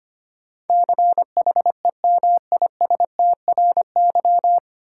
Morse code, heard as a keyed tone, about 25 wpm, 700 Hz. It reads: CHEMISTRY
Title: Sentences from Top 1000 Words - Review Part 6 - New Word Drill - 25wpm